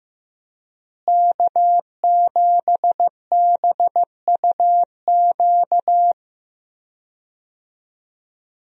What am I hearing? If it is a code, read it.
Morse code, heard as a keyed tone, about 16 wpm, 700 Hz. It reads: K7BUQ
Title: USA Call Signs - 5 Characters - Repeated 2x - 15wpm